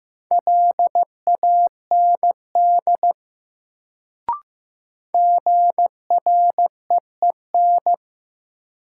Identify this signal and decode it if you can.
Morse code, heard as a keyed tone, about 15 wpm, 700 Hz. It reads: LAND GREEN
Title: Sentences from Top 500 Words - New Word Drill - 15wpm